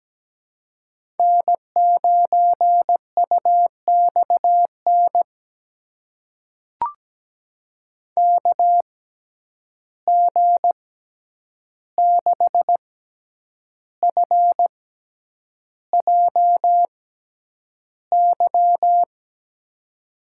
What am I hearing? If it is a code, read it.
Morse code, heard as a keyed tone, about 17 wpm, spaced as for 8 wpm, 700 Hz. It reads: N9UXN KG6FJY
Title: USA Call Signs - 6x Character Spacing - 17wpm